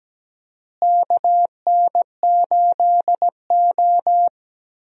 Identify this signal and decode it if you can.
Morse code, heard as a keyed tone, about 17 wpm, 700 Hz. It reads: KN8O
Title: USA Call Signs - 4 Characters - Repeated 2x - 17wpm